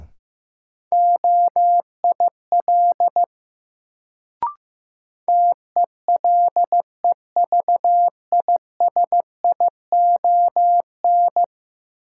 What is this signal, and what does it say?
Morse code, heard as a keyed tone, about 15 wpm, 700 Hz. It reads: OIL TELEVISION
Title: Sentences from Top 1000 Words - Review Part 4 - New Word Drill - 15wpm